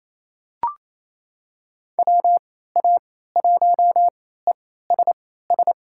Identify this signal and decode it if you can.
Morse code, heard as a keyed tone, about 27 wpm, spaced as for 16 wpm, 700 Hz. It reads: WA1ESS
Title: USA Call Signs - 3x Character Spacing - 28wpm